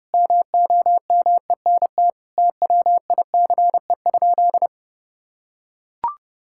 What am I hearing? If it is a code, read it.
Morse code, heard as a keyed tone, about 32 wpm, 700 Hz. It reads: MOMENT TWICE?
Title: Sentences from Top 1600 Words 30wpm